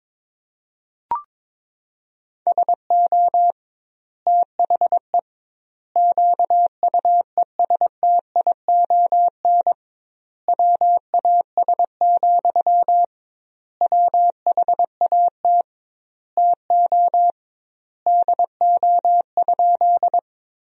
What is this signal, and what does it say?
Morse code, heard as a keyed tone, about 22 wpm, 700 Hz. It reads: SO THE QUESTION WAS, WHAT TO DO?
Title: 2x Word-Spacing - Sentences from Top 400 Words - 22wpm